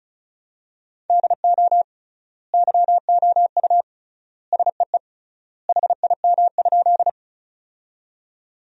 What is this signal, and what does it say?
Morse code, heard as a keyed tone, about 34 wpm, 700 Hz. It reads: DO YOU SEE HIM?
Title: 3x Word-Spacing - Sentences from Top 100 Words - 35wpm